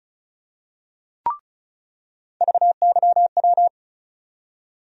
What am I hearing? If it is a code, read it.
Morse code, heard as a keyed tone, about 36 wpm, 700 Hz. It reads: VYW
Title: USA Call Sign Postfixes - 3 Letters 35wpm